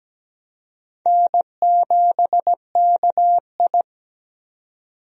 Morse code, heard as a keyed tone, about 17 wpm, 700 Hz. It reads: N7KI
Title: USA Call Signs - 4 Characters - 17wpm